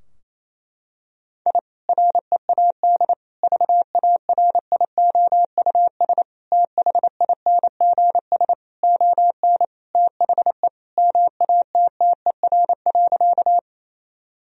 Morse code, heard as a keyed tone, about 28 wpm, 700 Hz. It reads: I READ VARIOUS THINGS ON THE MATTER.
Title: Sentences from Top 1000 Words - Review Part 4 - 28wpm